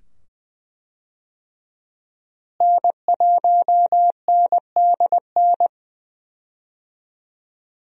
Morse code, heard as a keyed tone, about 20 wpm, 700 Hz. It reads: N1NDN
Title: USA Call Signs - 5 Characters - Repeated 2x - 20wpm